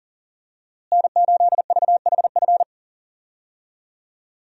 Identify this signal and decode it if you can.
Morse code, heard as a keyed tone, about 40 wpm, 700 Hz. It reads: N8VHF
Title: USA Call Signs - 5 Characters - Repeated 2x - 40wpm